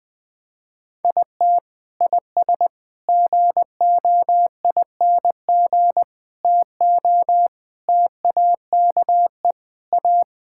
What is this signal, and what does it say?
Morse code, heard as a keyed tone, about 20 wpm, 700 Hz. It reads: IT IS GOING TO TAKE A
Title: Sentences with 4-Word Phrases - 20wpm